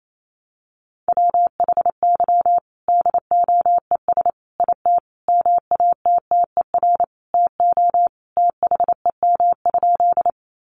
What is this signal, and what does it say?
Morse code, heard as a keyed tone, about 28 wpm, 700 Hz. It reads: WHY DOES IT MATTER TO THEM?
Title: Sentences from Top 700 Words 28wpm